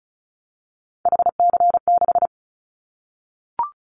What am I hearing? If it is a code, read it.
Morse code, heard as a keyed tone, about 34 wpm, 700 Hz. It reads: HC6
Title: Callsign Prefixes for Top 100 DXCC - 35wpm